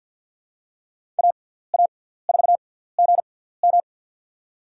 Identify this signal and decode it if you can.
Morse code, heard as a keyed tone, about 53 wpm, spaced as for 17 wpm, 700 Hz. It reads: AA4GM
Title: USA Call Signs - 5 Characters - 6x Character Spacing - 50wpm